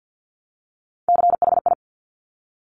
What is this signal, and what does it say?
Morse code, heard as a keyed tone, about 50 wpm, 690 Hz. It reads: CHI
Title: Top Three Letter Combinations in Words 50wpm